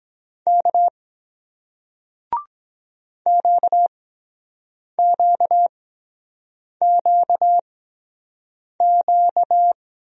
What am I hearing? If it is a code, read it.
Morse code, heard as a keyed tone, about 23 wpm, spaced as for 7 wpm, 700 Hz. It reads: K QQQQ